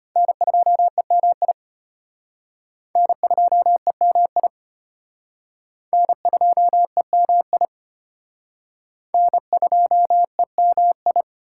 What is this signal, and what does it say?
Morse code, heard as a keyed tone, about 34 wpm, 700 Hz. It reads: N2EMI N2EMI N2EMI N2EMI